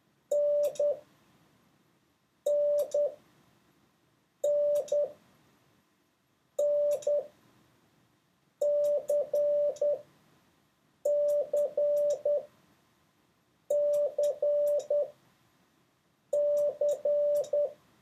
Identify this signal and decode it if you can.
Morse code, heard as a keyed tone, about 10 wpm, spaced as for 5 wpm, 580 Hz. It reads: NNNNCCCC
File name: 3of6-mix-and-match